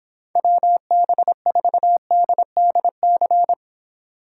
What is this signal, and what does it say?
Morse code, heard as a keyed tone, about 26 wpm, 700 Hz. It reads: WB4DDC